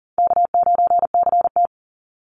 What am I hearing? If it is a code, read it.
Morse code, heard as a keyed tone, about 40 wpm, 700 Hz. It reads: K9CT
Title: Easy Call Signs - 40wpm